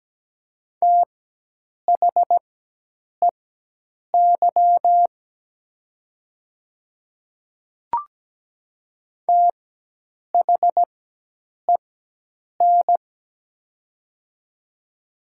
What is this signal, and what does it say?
Morse code, heard as a keyed tone, about 17 wpm, spaced as for 8 wpm, 700 Hz. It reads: THEY THEN
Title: 4x Character Spacing - Top 100 Words - 17wpm